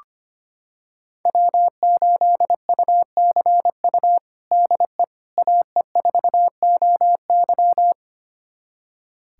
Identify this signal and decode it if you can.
Morse code, heard as a keyed tone, about 25 wpm, 700 Hz. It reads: W8UCU DE AE4OY
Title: USA Callsign DE Callsign 25wpm